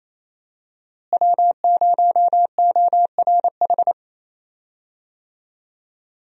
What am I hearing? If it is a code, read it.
Morse code, heard as a keyed tone, about 27 wpm, 700 Hz. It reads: W0ORH
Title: USA Call Signs - 5 Characters - Repeated 3x - 28wpm